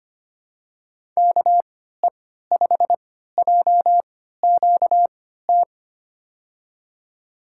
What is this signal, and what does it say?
Morse code, heard as a keyed tone, about 25 wpm, spaced as for 14 wpm, 700 Hz. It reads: KE5JQT
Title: USA Call Signs - 6 Characters - 3x Character Spacing - 25wpm